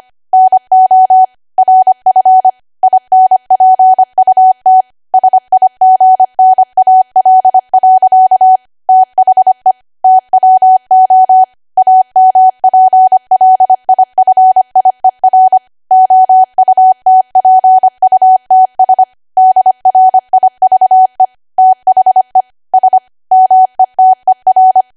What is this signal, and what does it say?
Morse code, heard as a keyed tone, about 25 wpm, 750 Hz. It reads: NO RF INPUT SIGNAL. THE TWO AMPLIFIER OUTPUTS DRIVE THE S METER
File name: Archive / 25 WPM / 25WPM